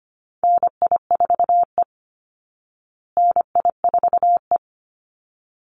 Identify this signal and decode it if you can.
Morse code, heard as a keyed tone, about 25 wpm, 700 Hz. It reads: NI4E NI4E
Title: Call Signs Repeated 3x - 25wpm